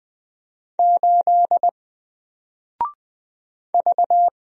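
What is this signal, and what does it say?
Morse code, heard as a keyed tone, about 20 wpm, 700 Hz. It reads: 8 V
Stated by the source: Single letter and number Morse drill at 20 wpm, from N4LCD.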